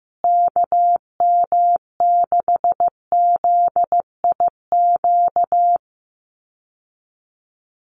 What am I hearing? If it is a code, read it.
Morse code, heard as a keyed tone, about 15 wpm, 700 Hz. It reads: KM6ZIQ
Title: USA Call Signs Repeated 2x 15wpm